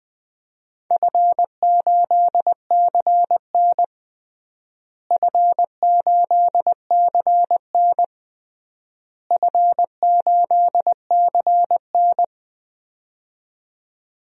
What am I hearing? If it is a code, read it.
Morse code, heard as a keyed tone, about 20 wpm, 700 Hz. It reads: F8CN F8CN F8CN
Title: DXCC-100 Call Signs - 4 Characters - Repeated 3x - 20wpm